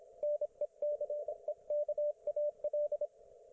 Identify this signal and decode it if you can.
Morse code, heard as a keyed tone, about 25 wpm, 590 Hz, with radio noise. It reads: NECEKAL
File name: Archiv / CUC194 / ok1fhi cul194